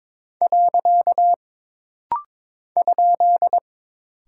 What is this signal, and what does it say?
Morse code, heard as a keyed tone, about 22 wpm, 700 Hz. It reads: . ?
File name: practice / 22wpm / Single Punctuation-Prosign - Rapid-Fire - 22wpm